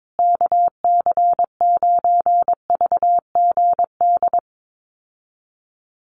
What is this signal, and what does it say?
Morse code, heard as a keyed tone, about 22 wpm, 700 Hz. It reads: KC9VGD
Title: USA Call Signs Repeated 3x 22wpm